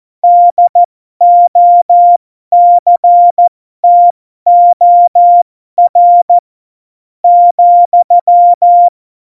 Morse code, heard as a keyed tone, about 14 wpm, spaced as for 11 wpm, 700 Hz. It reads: DOCTOR,
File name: Morsekurs / Witze